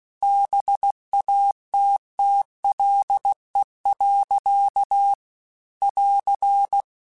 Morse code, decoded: BATTLE. <AR>

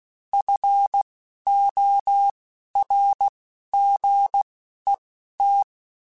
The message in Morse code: FORGET

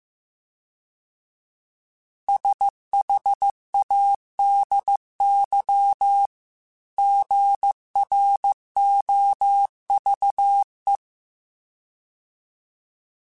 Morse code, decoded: SHADY GROVE

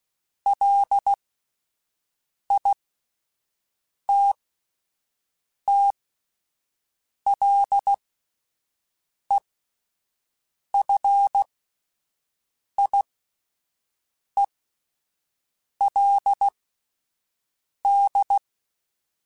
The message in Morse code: LITTLEFIELD

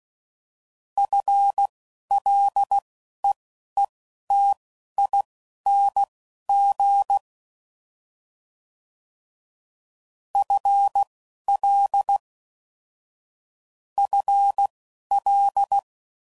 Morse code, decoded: FLEETING FL FL

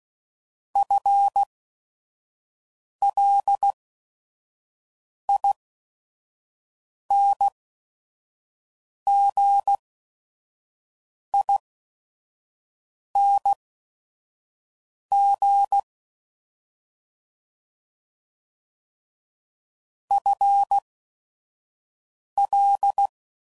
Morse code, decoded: FLINGING FL

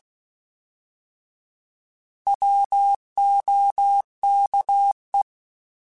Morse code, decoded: WOKE

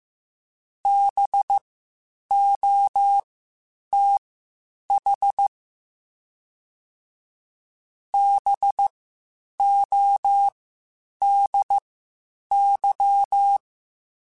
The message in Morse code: BOTH BODY

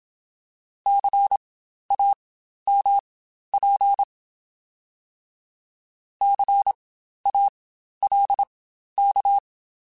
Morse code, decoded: CAMP CALK

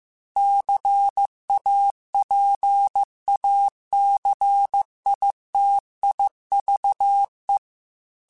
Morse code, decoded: CAPACITIVE

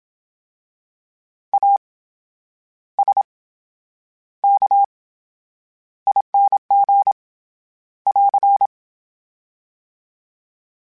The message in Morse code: ASKING <AR>

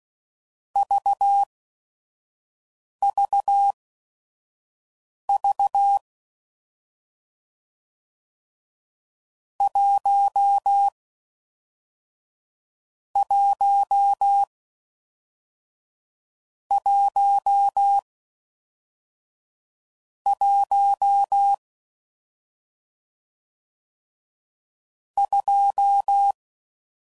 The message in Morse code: VVV 1 1 1 1 2

